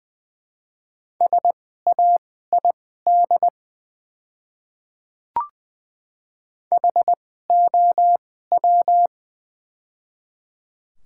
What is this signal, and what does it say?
Morse code, heard as a keyed tone, about 20 wpm, spaced as for 15 wpm, 700 Hz. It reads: SAID HOW
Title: 2x Character Spacing - Top 100 Words - 20wpm